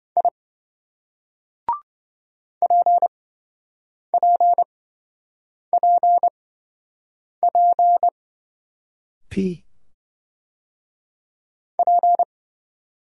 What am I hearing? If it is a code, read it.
Morse code, heard as a keyed tone, about 26 wpm, spaced as for 7 wpm, 700 Hz. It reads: I PPPP P